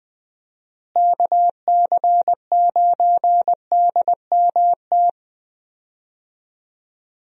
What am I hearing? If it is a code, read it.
Morse code, heard as a keyed tone, about 21 wpm, 700 Hz. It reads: KC9DMT
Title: USA Call Signs - 6 Characters - Repeated 2x - 20wpm